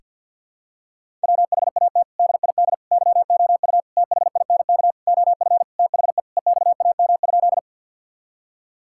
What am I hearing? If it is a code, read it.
Morse code, heard as a keyed tone, about 52 wpm, 700 Hz. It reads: WHAT DID YOU THINK OF THE EXAM?
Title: Sentences from Top 1000 Words - Review Part 5 - 50wpm